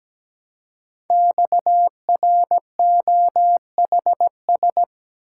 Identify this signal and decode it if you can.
Morse code, heard as a keyed tone, about 17 wpm, 700 Hz. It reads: XROHS
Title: Code Groups - 5 Letters - 17wpm